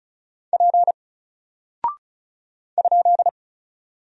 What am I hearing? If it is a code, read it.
Morse code, heard as a keyed tone, about 36 wpm, 700 Hz. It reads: P ?
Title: Single Character (Letter:Number:Punctuation:Prosign) Alt 35wpm